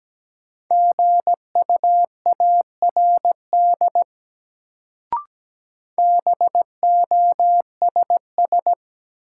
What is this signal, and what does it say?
Morse code, heard as a keyed tone, about 17 wpm, 700 Hz. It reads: GUARD BOSS
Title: Sentences from Top 700 Words - New Word Drill - 17wpm